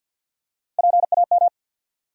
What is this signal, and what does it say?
Morse code, heard as a keyed tone, about 50 wpm, 700 Hz. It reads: PAM
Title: USA Common Names - Max 3 Letters - 50wpm